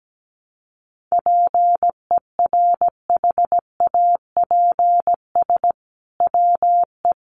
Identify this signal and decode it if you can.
Morse code, heard as a keyed tone, about 17 wpm, 700 Hz. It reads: PERHAPS WE
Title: Sentences from Top 1100 Words 17wpm